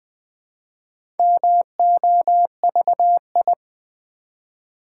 Morse code, heard as a keyed tone, about 20 wpm, 700 Hz. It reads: MOVI
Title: Build A Word - 1 Letter At A Time - 5-Letter Words - 20wpm